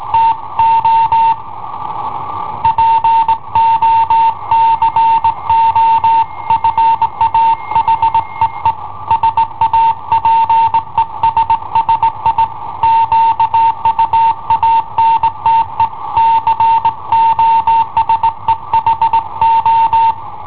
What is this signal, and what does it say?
Morse code, heard as a keyed tone, about 19 wpm, 910 Hz. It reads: TO POCO FA HEE SAPESSI QUANTE COSE HO
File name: i7ohp / cassetta i7ohp B